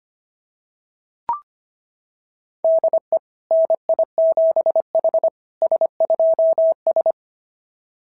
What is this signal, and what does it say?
Morse code, heard as a keyed tone, about 25 wpm, 690 Hz. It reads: DE NI7H S2S
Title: SOTA QSO Practice 25wpm